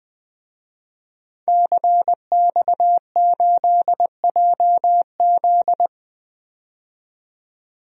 Morse code, heard as a keyed tone, about 20 wpm, 700 Hz. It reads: CX8JZ
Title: DXCC-100 Call Signs - 5 Characters - 20wpm